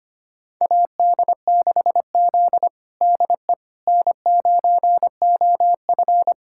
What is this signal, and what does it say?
Morse code, heard as a keyed tone, about 25 wpm, 700 Hz. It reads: AD6Z DE N9OF
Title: USA Callsign DE Callsign - 4 Characters - 25wpm